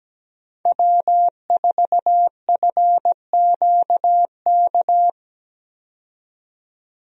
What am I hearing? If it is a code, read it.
Morse code, heard as a keyed tone, about 17 wpm, 700 Hz. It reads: W4FQK
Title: USA Call Signs - 5 Characters - 17wpm